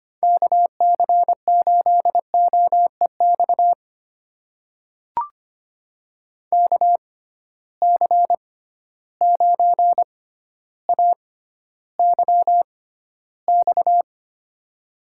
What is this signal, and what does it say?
Morse code, heard as a keyed tone, about 25 wpm, spaced as for 13 wpm, 700 Hz. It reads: KC8OEX KC9AYX
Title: USA Call Signs - 6 Characters - 6x Character Spacing - 25wpm